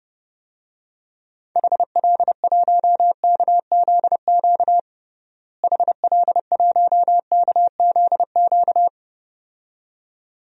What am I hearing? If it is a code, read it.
Morse code, heard as a keyed tone, about 32 wpm, 700 Hz. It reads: HL1KZQ HL1KZQ
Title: DXCC-100 Call Signs - 6 Characters - Repeated 2x - 30wpm